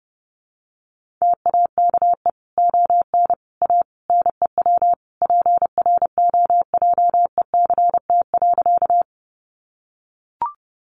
Morse code, heard as a keyed tone, about 28 wpm, 700 Hz. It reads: TAKE ON A NEW PROJECT.
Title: Sentences from Top 1400 Words 30wpm